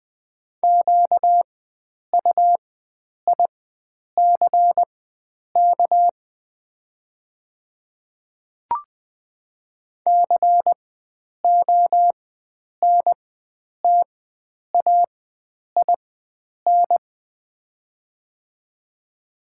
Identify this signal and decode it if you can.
Morse code, heard as a keyed tone, about 20 wpm, spaced as for 9 wpm, 690 Hz. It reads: QUICK CONTAIN
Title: 4x Character Spacing - Top 500 Words - 20wpm